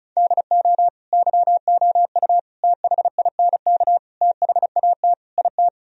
Morse code, heard as a keyed tone, about 36 wpm, 700 Hz. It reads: DO YOU THINK THAT IT
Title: Sentences from Top 1000 Words - Review Part 4 - 35wpm